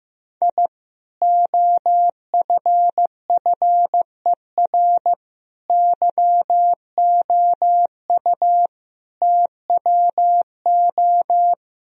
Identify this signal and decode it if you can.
Morse code, heard as a keyed tone, about 15 wpm, 700 Hz. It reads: I OFFER YOU TWO
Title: Sentences from Top 1000 Words - Review Part 4 - 15wpm